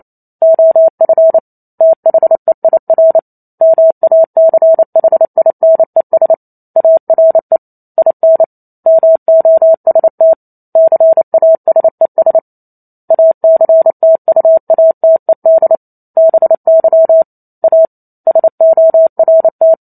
Morse code, heard as a keyed tone, about 28 wpm, 650 Hz. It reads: OF THEIR MACHINES ARE IN MOST CASES ACTUATED BY A SORT
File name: Chapter 2 - 28 WPM